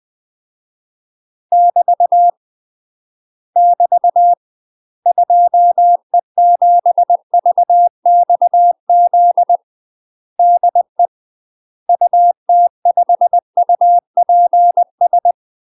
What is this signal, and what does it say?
Morse code, heard as a keyed tone, about 20 wpm, 700 Hz. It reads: <BT> <BT> 2E7VXZ DE UT5UPS
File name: UK MASTER QSOs TRAINING / 20WPM / UK MASTER QSOs 0039